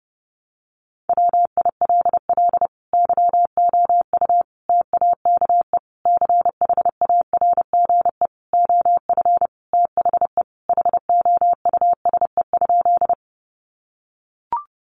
Morse code, heard as a keyed tone, about 30 wpm, 700 Hz. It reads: WILL YOU TAKE CHARGE OF THE HOUSE?
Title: Sentences from Top 900 Words 30wpm